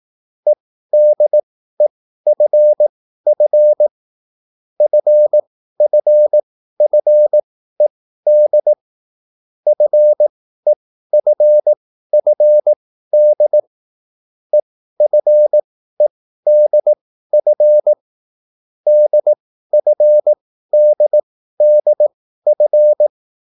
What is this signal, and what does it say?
Morse code, heard as a keyed tone, about 18 wpm, spaced as for 13 wpm, 600 Hz. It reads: EDEFF FFFED FEFFD EFEDF DFDDF
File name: QSL / CW lesson